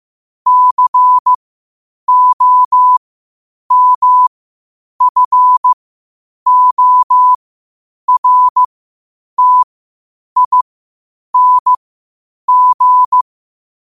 Morse code, heard as a keyed tone, about 15 wpm, spaced as for 8 wpm, 1000 Hz. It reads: COMFORTING
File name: qotdfiles / 8wpm